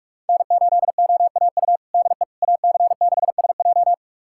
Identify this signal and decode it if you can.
Morse code, heard as a keyed tone, about 43 wpm, 700 Hz. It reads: N8OAU DE AC6SJ
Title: USA Callsign DE Callsign - 5 Characters - 45wpm